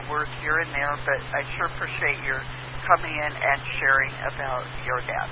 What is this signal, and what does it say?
Emergency Alert System (SAME) tones, heard as harsh buzzing bursts, with speech. It is heard from 0.4 to 4.1 s.